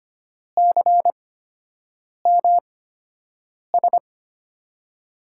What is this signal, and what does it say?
Morse code, heard as a keyed tone, about 25 wpm, spaced as for 7 wpm, 700 Hz. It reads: CMS
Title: USA Call Signs - 6 Characters - 8x Character Spacing - 25wpm